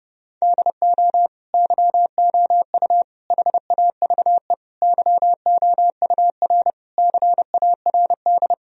Morse code, heard as a keyed tone, about 30 wpm, 700 Hz. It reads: DO YOU HAVE YOUR CARD
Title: Sentences from Top 1000 Words - Review Part 4 - 30wpm